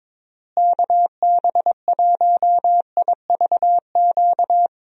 Morse code, heard as a keyed tone, about 22 wpm, 700 Hz. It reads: KB1IVQ
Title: USA Call Signs - 6 Characters - Repeated 3x - 22wpm